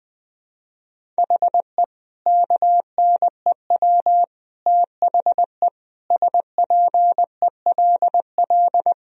Morse code, heard as a keyed tone, about 21 wpm, 700 Hz. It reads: HE KNEW THE SPELL
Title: Sentences from Top 1000 Words - Review Part 1 - 20wpm